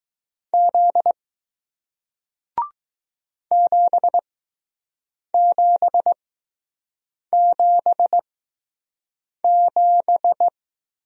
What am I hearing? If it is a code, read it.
Morse code, heard as a keyed tone, about 20 wpm, spaced as for 6 wpm, 700 Hz. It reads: Z 7777